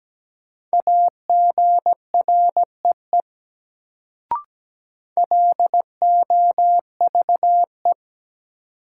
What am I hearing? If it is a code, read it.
Morse code, heard as a keyed tone, about 17 wpm, 700 Hz. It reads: AGREE LOVE
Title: Sets of 5 Words Encore - New Word Drill - 17wpm